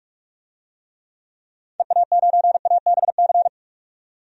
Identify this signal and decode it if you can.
Morse code, heard as a keyed tone, about 46 wpm, 700 Hz. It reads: EA9ABC
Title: DXCC-100 Call Signs - 6 Characters - 45wpm